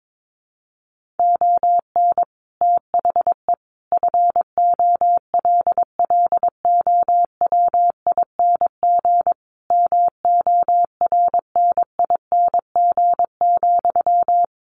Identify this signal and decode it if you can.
Morse code, heard as a keyed tone, about 22 wpm, 700 Hz. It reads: ON THE FOLLOWING MORNING,
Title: Sentences from Top 1400 Words 22wpm